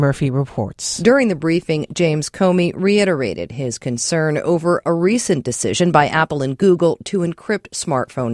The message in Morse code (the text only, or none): none